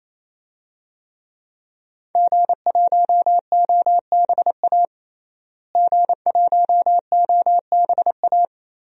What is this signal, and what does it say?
Morse code, heard as a keyed tone, about 28 wpm, 700 Hz. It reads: G1OBA G1OBA
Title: DXCC-100 Call Signs - 5 Characters - Repeated 3x - 28wpm